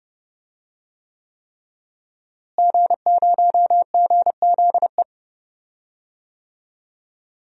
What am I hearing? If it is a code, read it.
Morse code, heard as a keyed tone, about 30 wpm, 700 Hz. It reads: G0GZE
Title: DXCC-100 Call Signs - 5 Characters - Repeated 3x - 30wpm